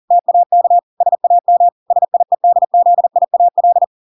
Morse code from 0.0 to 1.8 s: TAK SAM